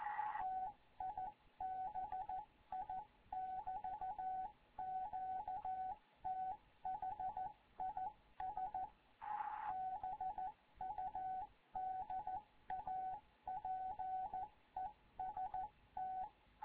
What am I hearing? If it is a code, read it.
Morse code, heard as a keyed tone, about 14 wpm, 750 Hz. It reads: TIBI <BT> QTH IS BUDAPEST
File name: Morze-HA2MN / hangp / 2 11 GYAKORLAT